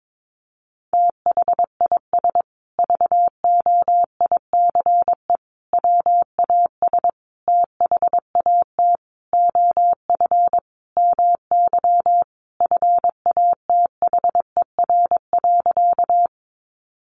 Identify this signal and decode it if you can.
Morse code, heard as a keyed tone, about 22 wpm, 700 Hz. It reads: THIS VOICE WAS THAT OF MY FATHER.
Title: Sentences from Top 500 Words 22wpm